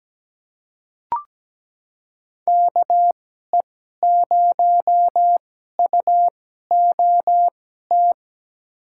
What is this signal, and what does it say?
Morse code, heard as a keyed tone, about 17 wpm, spaced as for 12 wpm, 700 Hz. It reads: KE0UOT
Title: USA Call Signs - 6 Characters - 2x Character Spacing - 17wpm